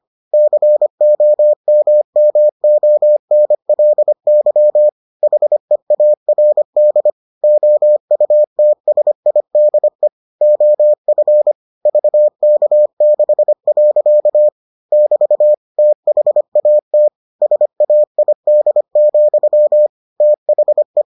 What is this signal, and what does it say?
Morse code, heard as a keyed tone, about 25 wpm, 600 Hz. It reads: COMMONLY HEARD OUTSIDE OF VK6. <BT> THAT SAID, THE